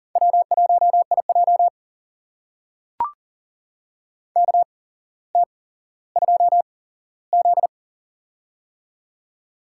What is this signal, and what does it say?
Morse code, heard as a keyed tone, about 40 wpm, spaced as for 17 wpm, 700 Hz. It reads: W1IJ KT2Z
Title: USA Call Signs - 4 Characters - 8x Character Spacing - 40wpm